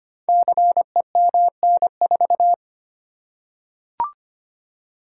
Morse code, heard as a keyed tone, about 25 wpm, 700 Hz. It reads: CEMN4
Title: Code Groups - 5 Letter:Numbers - 25wpm